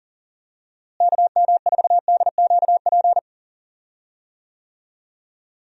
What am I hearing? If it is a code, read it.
Morse code, heard as a keyed tone, about 40 wpm, 700 Hz. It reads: KM4DQP
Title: USA Call Signs - 6 Characters - Repeated 3x - 40wpm